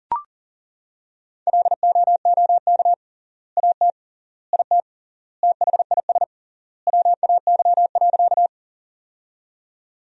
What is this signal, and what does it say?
Morse code, heard as a keyed tone, about 40 wpm, 700 Hz. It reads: LOOK AT IT THIS WAY.
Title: 3x Word-Spacing - Sentences from Top 100 Words - 40wpm